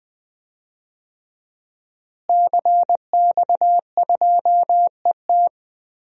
Morse code, heard as a keyed tone, about 20 wpm, 700 Hz. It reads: CX2ET
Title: DXCC-100 Call Signs - 5 Characters - Repeated 3x - 20wpm